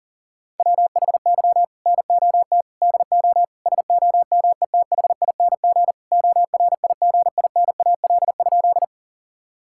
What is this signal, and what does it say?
Morse code, heard as a keyed tone, about 37 wpm, 700 Hz. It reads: WHY NOT DO SOMETHING ORIGINAL?
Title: Sentences from Top 1000 Words - Review Part 6 - 40wpm